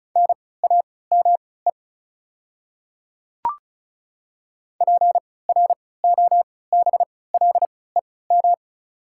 Morse code, heard as a keyed tone, about 34 wpm, spaced as for 20 wpm, 700 Hz. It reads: NAME PROBLEM
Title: 3x Character Spacing - Top 500 Words - 35wpm